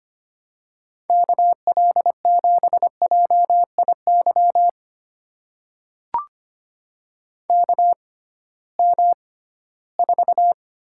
Morse code, heard as a keyed tone, about 25 wpm, spaced as for 16 wpm, 700 Hz. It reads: KL7JIY KM4
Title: USA Call Signs - 6 Characters - 6x Character Spacing - 25wpm